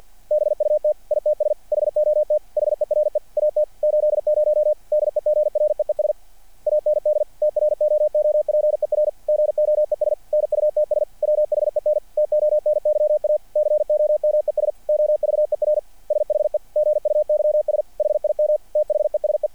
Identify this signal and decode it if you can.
Morse code, heard as a keyed tone, about 50 wpm, 600 Hz. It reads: BUT ITS HOT HERE AT 80 DEGREES AND TROOPER GOES NUTS WHEN TONYA COMES OVER SHE GUYS HIM THESE
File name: archived / CRSnet